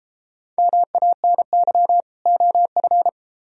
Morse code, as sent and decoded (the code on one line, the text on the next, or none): -- .- -. -.-- / --- ..-.
MANY OF